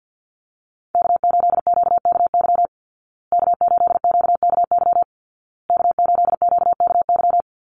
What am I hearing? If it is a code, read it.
Morse code, heard as a keyed tone, about 50 wpm, 700 Hz. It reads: K8QKY K8QKY K8QKY
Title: Call Signs Repeated 2x - 50wpm